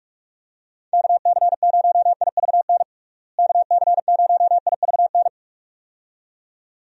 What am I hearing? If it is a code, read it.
Morse code, heard as a keyed tone, about 45 wpm, 700 Hz. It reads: KC0IVN KC0IVN
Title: USA Call Signs - 6 Characters - Repeated 2x - 45wpm